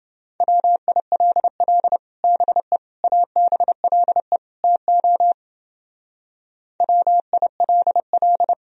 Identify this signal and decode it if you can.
Morse code, heard as a keyed tone, about 29 wpm, 700 Hz. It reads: WILL BE ABLE TO WILL